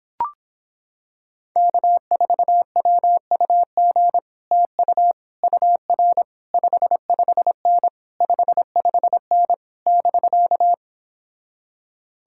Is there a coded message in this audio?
Yes: Morse code from 1.6 to 10.7 s, heard as a keyed tone, about 26 wpm, 700 Hz. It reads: K4WUG TU UR 55N 55N <BK>